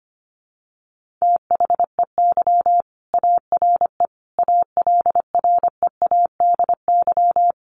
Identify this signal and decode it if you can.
Morse code, heard as a keyed tone, about 25 wpm, 700 Hz. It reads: THEY ARE ALREADY
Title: Sentences from Top 1500 Words 25wpm